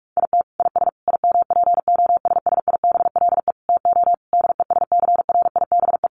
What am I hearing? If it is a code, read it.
Morse code, heard as a keyed tone, about 44 wpm, 700 Hz. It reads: IT IS IMPOSSIBLE TO DESCRIBE
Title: Sentences from Top 1100 Words 45wpm